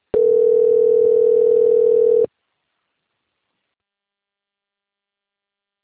Ringback tone (North America), 440 + 480 Hz.